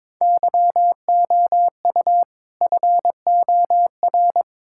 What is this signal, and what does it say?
Morse code, heard as a keyed tone, about 22 wpm, 700 Hz. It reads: YOU FOR
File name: practice / 22wpm / Build A Phrase - 5-Word Phrases - 22wpm